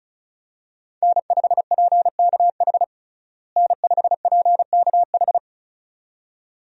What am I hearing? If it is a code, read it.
Morse code, heard as a keyed tone, about 36 wpm, 700 Hz. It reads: N5PKH N5PKH
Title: USA Call Signs - 5 Characters - Repeated 2x - 35wpm